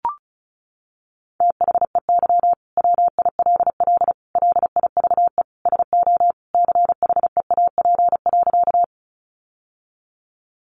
Morse code, heard as a keyed tone, about 34 wpm, 700 Hz. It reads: THEY WILL LIVE SO CHEAP.
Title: Sentences from Top 1200 Words 35wpm